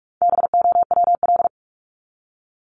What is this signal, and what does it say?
Morse code, heard as a keyed tone, about 44 wpm, 700 Hz. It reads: BOWL
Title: Top 1001-1250 Words 45wpm